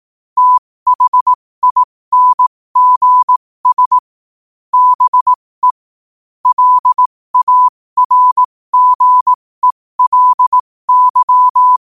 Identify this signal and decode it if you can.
Morse code, heard as a keyed tone, about 18 wpm, spaced as for 15 wpm, 1000 Hz. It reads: THINGS BE LARGELY
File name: qotdfiles / 15wpm